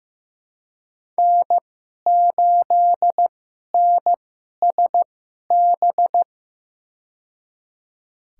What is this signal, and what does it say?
Morse code, heard as a keyed tone, about 15 wpm, spaced as for 11 wpm, 700 Hz. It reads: N8NSB
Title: USA Call Signs - 5 Characters - 2x Character Spacing - 15wpm